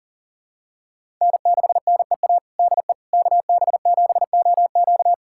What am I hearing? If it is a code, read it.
Morse code, heard as a keyed tone, about 40 wpm, 700 Hz. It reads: N6NEA DE KB7OQ
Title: USA Callsign DE Callsign - 5 Characters - 40wpm